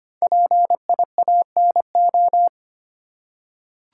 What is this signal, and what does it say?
Morse code, heard as a keyed tone, about 25 wpm, 700 Hz. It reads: PIANO
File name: practice / 25wpm / Easy Words with Hints - 25wpm